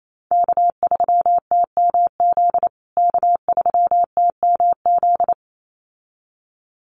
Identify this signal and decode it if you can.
Morse code, heard as a keyed tone, about 28 wpm, 700 Hz. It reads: K3TMZ K3TMZ
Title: USA Call Signs Repeated 2x 28wpm